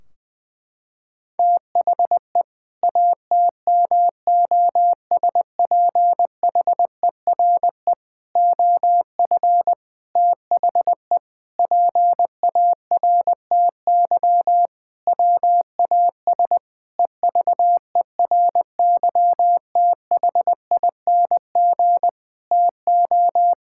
Morse code, heard as a keyed tone, about 21 wpm, 700 Hz. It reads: THE ATMOSPHERE OF THE PARTY WAS EVERYTHING TO
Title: Sentences from Top 1600 Words 20wpm